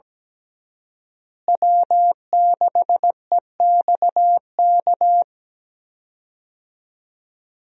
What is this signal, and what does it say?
Morse code, heard as a keyed tone, about 17 wpm, 700 Hz. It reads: W6EXK
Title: USA Call Signs - 5 Characters - Repeated 2x - 17wpm